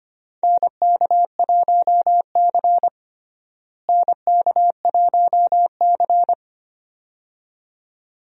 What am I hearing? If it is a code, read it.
Morse code, heard as a keyed tone, about 25 wpm, 700 Hz. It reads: NK1C NK1C
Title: USA Call Signs - 4 Characters - Repeated 2x - 25wpm